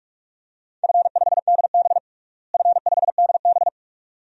Morse code, heard as a keyed tone, about 42 wpm, 700 Hz. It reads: F5DB F5DB
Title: DXCC-100 Call Signs - 4 Characters - Repeated 2x - 45wpm